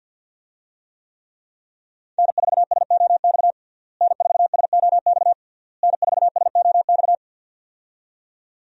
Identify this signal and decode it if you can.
Morse code, heard as a keyed tone, about 47 wpm, 700 Hz. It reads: N4SOX N4SOX N4SOX
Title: USA Call Signs - 5 Characters - Repeated 3x - 50wpm